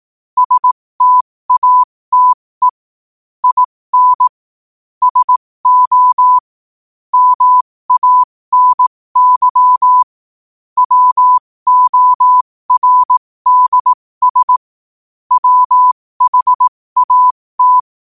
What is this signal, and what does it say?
Morse code, heard as a keyed tone, about 18 wpm, spaced as for 15 wpm, 1000 Hz. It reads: STATE IN SO MANY WORDS WHAT
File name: qotdfiles / 15wpm